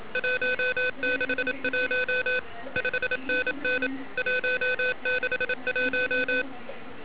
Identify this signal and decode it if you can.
Morse code, heard as a keyed tone, about 28 wpm, 1500 Hz, with noise. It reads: 161 5NN 161
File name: ARCHIV / 80m cw